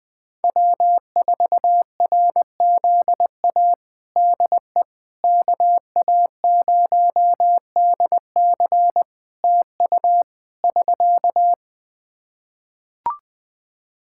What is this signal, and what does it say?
Morse code, heard as a keyed tone, about 20 wpm, 700 Hz. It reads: W4RZA DE KA0DC TU <SK>
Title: Practice QSO - Exchange 4 - Ending - 20wpm